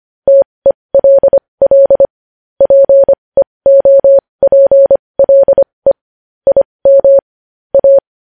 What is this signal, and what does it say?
Morse code, heard as a keyed tone, about 25 wpm, spaced as for 20 wpm, 550 Hz. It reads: TELL PEOPLE IM A
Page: news-headlines-in-morse-code-at-20-wpm